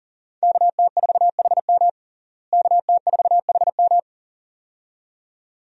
Morse code, heard as a keyed tone, about 37 wpm, 700 Hz. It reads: KT4HM KT4HM
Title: USA Call Signs - 5 Characters - Repeated 2x - 40wpm